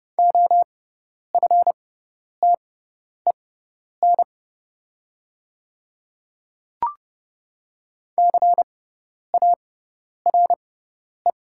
Morse code, heard as a keyed tone, about 30 wpm, spaced as for 10 wpm, 700 Hz. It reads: OFTEN CARE
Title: 6x Character Spacing - Top 300 Words - 30wpm